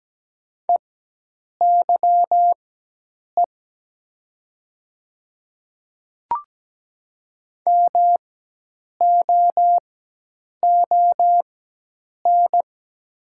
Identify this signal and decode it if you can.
Morse code, heard as a keyed tone, about 17 wpm, spaced as for 8 wpm, 700 Hz. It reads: EYE MOON
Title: 4x Character Spacing - Top 500 Words - 17wpm